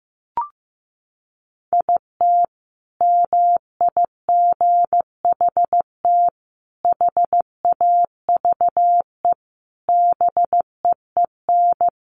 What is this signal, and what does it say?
Morse code, heard as a keyed tone, about 15 wpm, 700 Hz. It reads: IT MIGHT HAVE BEEN
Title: Sentences from Top 1200 Words 15wpm